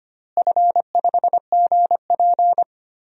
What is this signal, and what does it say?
Morse code, heard as a keyed tone, about 25 wpm, 700 Hz. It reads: F5GP
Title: DXCC-100 Call Signs - 4 Characters - Repeated 2x - 25wpm